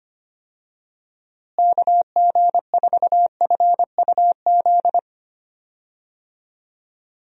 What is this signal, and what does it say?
Morse code, heard as a keyed tone, about 25 wpm, 700 Hz. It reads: KG4FUZ
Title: USA Call Signs - 6 Characters - Repeated 2x - 25wpm